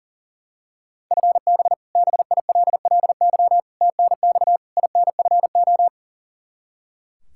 Morse code, heard as a keyed tone, about 40 wpm, 700 Hz. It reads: FB BILLY TNX INFO
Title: K1USN SST QSO Practice 40wpm